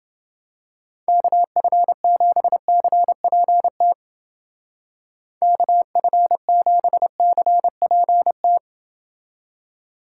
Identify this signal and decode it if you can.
Morse code, heard as a keyed tone, about 31 wpm, 700 Hz. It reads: KF7CPT KF7CPT